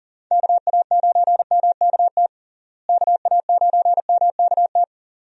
Morse code, heard as a keyed tone, about 40 wpm, 700 Hz. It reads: KA9MKT KA9MKT
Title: USA Call Signs - 6 Characters - Repeated 2x - 40wpm